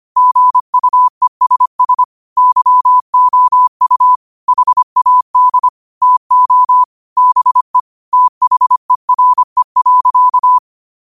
Morse code, heard as a keyed tone, about 25 wpm, 1000 Hz. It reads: GUESS YOU HAD TO BE THERE.
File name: qotdfiles / 25wpm